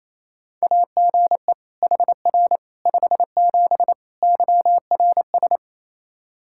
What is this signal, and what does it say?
Morse code, heard as a keyed tone, about 27 wpm, 700 Hz. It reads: AGE HR 57 YRS
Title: K1USN SST QSO Practice 28wpm